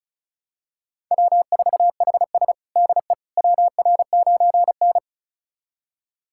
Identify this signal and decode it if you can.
Morse code, heard as a keyed tone, about 34 wpm, 700 Hz. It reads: W4HS DE WR9N
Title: USA Callsign DE Callsign - 4 Characters - 35wpm